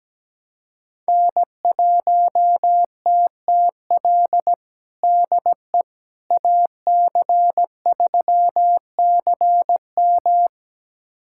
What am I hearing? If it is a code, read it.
Morse code, heard as a keyed tone, about 17 wpm, 700 Hz. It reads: N1TTL DE AC3CM
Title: USA Callsign DE Callsign - 5 Characters - 17wpm